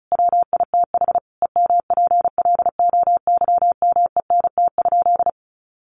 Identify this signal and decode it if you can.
Morse code, heard as a keyed tone, about 34 wpm, 700 Hz. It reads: WITH EMPLOYMENT?
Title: Sentences from Top 1500 Words 35wpm